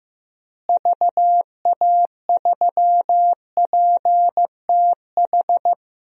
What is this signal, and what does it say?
Morse code, heard as a keyed tone, about 15 wpm, 700 Hz. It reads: VA3PTH
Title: DXCC-100 Call Signs - 6 Characters - Repeated 2x - 15wpm